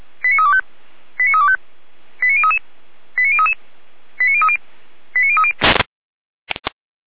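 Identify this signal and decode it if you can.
Five-tone selective call (ZVEI), heard as quick runs of five tones, tones of about 70 ms. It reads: 89325 89325 89033 89033 89030 89030
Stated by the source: ZVEI five-tone call received by SDR, from the Feuerwehr Orlamünde audio server.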